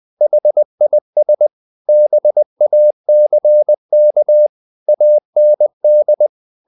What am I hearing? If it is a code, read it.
Morse code, heard as a keyed tone, about 20 wpm, 600 Hz. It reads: HIS BACK AND
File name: the old man and the sea / the old man and the sea part 1 20wpm